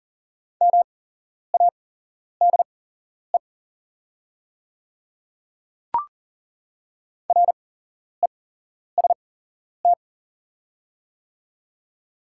Morse code, heard as a keyed tone, about 40 wpm, spaced as for 11 wpm, 690 Hz. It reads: MADE REST
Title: 8x Character Spacing - Top 500 Words - 40wpm